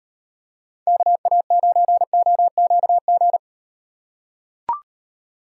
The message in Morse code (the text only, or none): KA9OQG